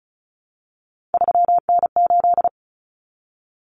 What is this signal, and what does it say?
Morse code, heard as a keyed tone, about 34 wpm, 700 Hz. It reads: 3N8
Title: Common Cut Numbers - 3-Digit 35wpm